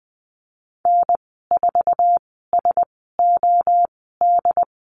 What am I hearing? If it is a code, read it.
Morse code, heard as a keyed tone, about 20 wpm, spaced as for 15 wpm, 700 Hz. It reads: N4SOD
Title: 069 Lesson 12g - Letter d - Call Signs 20wpm